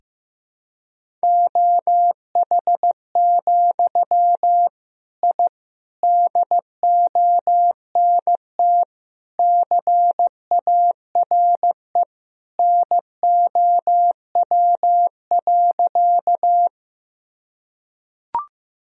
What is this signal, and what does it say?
Morse code, heard as a keyed tone, about 15 wpm, 700 Hz. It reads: OH, I DONT CARE NOW.
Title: Sentences from Top 1000 Words - Review Part 1 - 15wpm